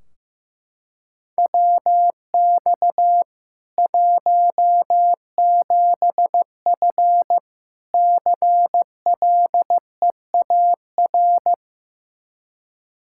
Morse code, heard as a keyed tone, about 15 wpm, 700 Hz. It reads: WX 17F CLEAR
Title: Practice QSO - Exchange 2 - Info - 15wpm